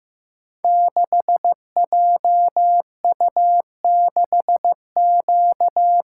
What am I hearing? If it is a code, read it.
Morse code, heard as a keyed tone, about 15 wpm, 700 Hz. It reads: 6JU6Q
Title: Code Groups - 5 Letter:Numbers - 15wpm